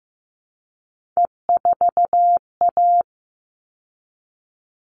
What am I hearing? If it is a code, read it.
Morse code, heard as a keyed tone, about 15 wpm, 700 Hz. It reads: E4A
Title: All Cut Numbers - 3-Digit 15wpm